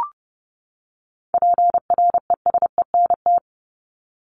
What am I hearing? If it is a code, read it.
Morse code, heard as a keyed tone, about 30 wpm, 700 Hz. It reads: PRESENT
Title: ED Suffix - Top 100 Words with Base Words 30wpm